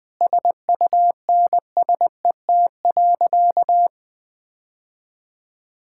Morse code, heard as a keyed tone, about 20 wpm, 700 Hz. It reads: SUNSET.
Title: Sentences with 6-Word Phrases - 20wpm